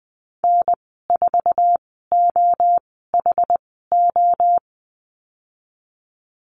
Morse code, heard as a keyed tone, about 20 wpm, spaced as for 14 wpm, 700 Hz. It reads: N4OHO
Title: 076 Lesson 13g - Letter L - Call Signs 20wpm